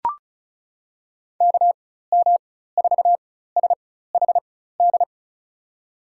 Morse code, heard as a keyed tone, about 34 wpm, spaced as for 16 wpm, 700 Hz. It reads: KM4SHD
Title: USA Call Signs - 6 Characters - 4x Character Spacing - 35wpm